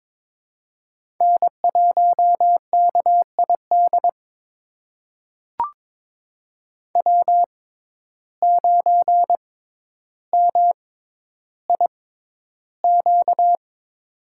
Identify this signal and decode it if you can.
Morse code, heard as a keyed tone, about 23 wpm, spaced as for 11 wpm, 700 Hz. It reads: N1KID W9MIQ
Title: USA Call Signs - 5 Characters - 6x Character Spacing - 22wpm